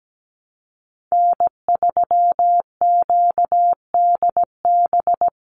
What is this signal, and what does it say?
Morse code, heard as a keyed tone, about 17 wpm, 700 Hz. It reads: N3QDB
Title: USA Call Signs 17wpm